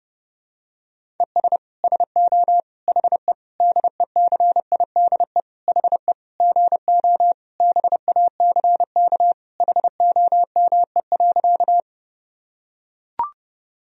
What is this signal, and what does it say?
Morse code, heard as a keyed tone, about 32 wpm, 700 Hz. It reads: ES SO HE DECIDE HE GO BACK HOME.
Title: Sentences from Top 500 Words and Abbreviations 30wpm